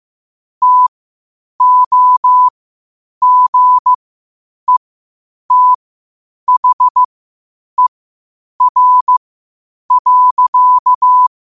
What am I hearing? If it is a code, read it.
Morse code, heard as a keyed tone, about 15 wpm, spaced as for 8 wpm, 1000 Hz. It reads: TOGETHER.
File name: qotdfiles / 8wpm